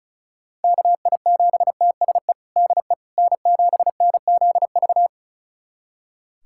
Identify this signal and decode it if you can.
Morse code, heard as a keyed tone, about 34 wpm, 700 Hz. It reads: KI7TSE DE N7NZV
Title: USA Callsign DE Callsign 35wpm